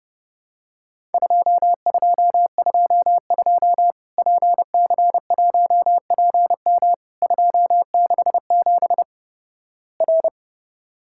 Morse code, heard as a keyed tone, about 32 wpm, 700 Hz. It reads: 2222 PC1PM 267 R
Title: Worked All Europe DX Contest - QTC Exchange - 30wpm